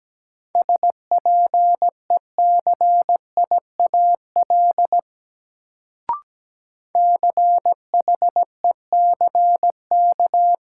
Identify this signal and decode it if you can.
Morse code, heard as a keyed tone, about 17 wpm, 700 Hz. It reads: SPECIAL CHECK
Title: Sentences from Top 1000 Words - Review Part 3 - New Word Drill - 17wpm